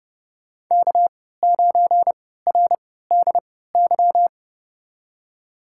Morse code, heard as a keyed tone, about 32 wpm, spaced as for 17 wpm, 700 Hz. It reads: K9RDY
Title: USA Call Signs - 5 Characters - 3x Character Spacing - 30wpm